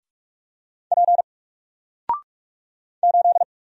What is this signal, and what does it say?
Morse code, heard as a keyed tone, about 46 wpm, 700 Hz. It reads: P 8
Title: Single Letter-Number 45wpm